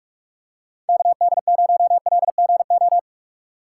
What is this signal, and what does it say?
Morse code, heard as a keyed tone, about 43 wpm, 700 Hz. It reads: KD0LGO
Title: USA Call Signs - 6 Characters - 2x Character Spacing - 45wpm